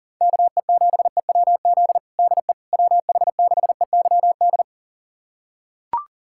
Morse code, heard as a keyed tone, about 37 wpm, 700 Hz. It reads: KE7EWZ DE WH6EYD
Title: USA Callsign DE Callsign 40wpm